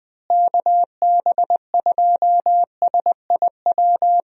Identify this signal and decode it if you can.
Morse code, heard as a keyed tone, about 20 wpm, 700 Hz. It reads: KB2SIW
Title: USA Call Signs - 6 Characters - Repeated 3x - 20wpm